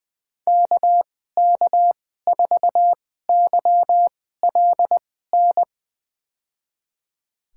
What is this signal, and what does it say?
Morse code, heard as a keyed tone, about 20 wpm, spaced as for 14 wpm, 700 Hz. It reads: KK4YLN
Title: USA Call Signs - 6 Characters - 2x Character Spacing - 20wpm